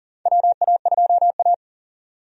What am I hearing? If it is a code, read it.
Morse code, heard as a keyed tone, about 37 wpm, 700 Hz. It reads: WA2A
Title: USA Call Signs - 4 Characters - 40wpm